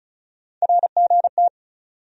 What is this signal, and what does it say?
Morse code, heard as a keyed tone, about 36 wpm, 700 Hz. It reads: RGT
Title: USA Call Sign Postfixes - 3 Letters 35wpm